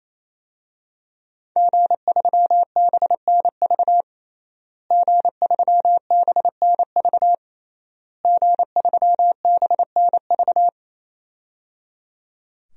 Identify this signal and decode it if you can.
Morse code, heard as a keyed tone, about 28 wpm, 700 Hz. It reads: G3BNV G3BNV G3BNV
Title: DXCC-100 Call Signs - 5 Characters - Repeated 3x - 28wpm